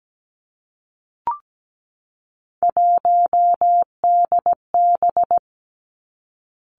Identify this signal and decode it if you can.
Morse code, heard as a keyed tone, about 17 wpm, 700 Hz. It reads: 1DB
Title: Three Challenging Letter-Number Combinations 17wpm